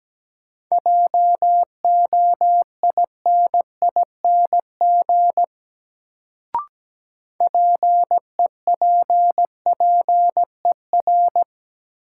Morse code, heard as a keyed tone, about 17 wpm, 700 Hz. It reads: JOINING PEPPER
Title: Sets of 5 Words - New Word Drill - 17wpm